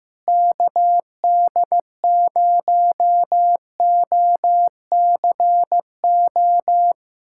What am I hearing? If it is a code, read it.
Morse code, heard as a keyed tone, about 15 wpm, 700 Hz. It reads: KD0OCO
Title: USA Callsign DE Callsign - 6 Characters - 15wpm